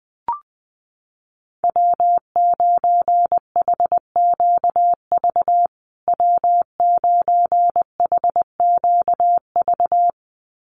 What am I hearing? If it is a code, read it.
Morse code, heard as a keyed tone, about 20 wpm, 700 Hz. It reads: W9HQV W9HQV
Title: USA Call Signs Repeated 2x 20wpm